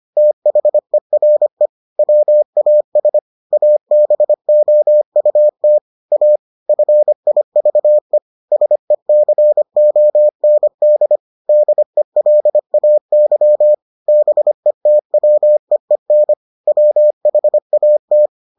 Morse code, heard as a keyed tone, about 25 wpm, 600 Hz. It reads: THERE WAS ABOUT A FIVE SECOND DELAY BETWEEN WHAT